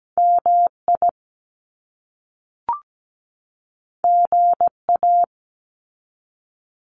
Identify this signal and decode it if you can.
Morse code, heard as a keyed tone, about 17 wpm, 700 Hz. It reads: MI GA
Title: US States 17wpm